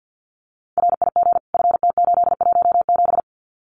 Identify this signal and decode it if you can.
Morse code, heard as a keyed tone, about 50 wpm, 700 Hz. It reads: RIG FT817
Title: Common QSO Phrases Encore 50wpm